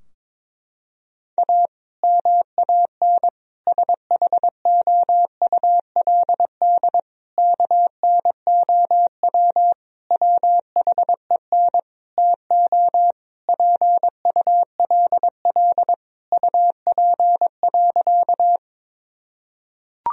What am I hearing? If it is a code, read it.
Morse code, heard as a keyed tone, about 22 wpm, 700 Hz. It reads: A MAN SHOULD KNOW WHEN TO PULL UP.
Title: Sentences from Top 1000 Words - Review Part 2 - 22wpm